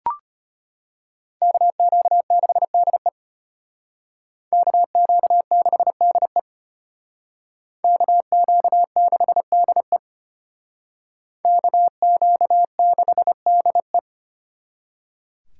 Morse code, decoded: KQ6DE KQ6DE KQ6DE KQ6DE

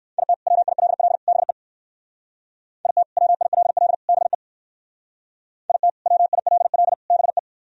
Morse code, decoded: IT WILL BE IT WILL BE IT WILL BE

